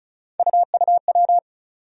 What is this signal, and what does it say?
Morse code, heard as a keyed tone, about 36 wpm, 700 Hz. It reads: UUW
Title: USA Call Sign Postfixes - 3 Letters 35wpm